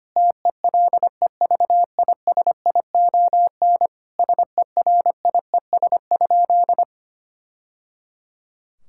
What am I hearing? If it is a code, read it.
Morse code, heard as a keyed tone, about 25 wpm, 700 Hz. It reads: TELEVISION SERIES?
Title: Sentences from Top 1000 Words - Review Part 5 - 25wpm